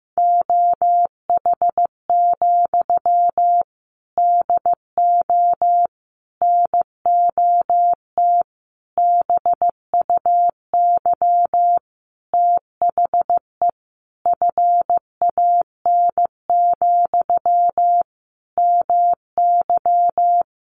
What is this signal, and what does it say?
Morse code, heard as a keyed tone, about 15 wpm, 700 Hz. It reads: OH, DO NOT BUY THE FAN, MY
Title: Sentences from Top 1300 Words 15wpm